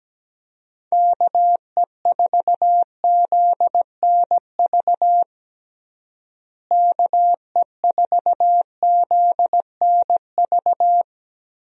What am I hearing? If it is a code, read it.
Morse code, heard as a keyed tone, about 17 wpm, 700 Hz. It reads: KE4ZNV KE4ZNV
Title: USA Call Signs - 6 Characters - Repeated 3x - 17wpm